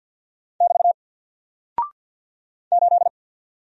Morse code, heard as a keyed tone, about 49 wpm, 700 Hz. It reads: <BT> 8
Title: Single Character (Letter:Number:Punctuation:Prosign) 50wpm